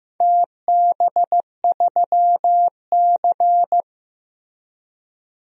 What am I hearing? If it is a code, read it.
Morse code, heard as a keyed tone, about 15 wpm, 700 Hz. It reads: TB3C
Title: DXCC-100 Call Signs - 4 Characters - Repeated 2x - 15wpm